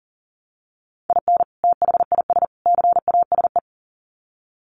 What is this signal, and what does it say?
Morse code, heard as a keyed tone, about 40 wpm, 700 Hz. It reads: IN THIS CASE
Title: Sets of 3 Words 40wpm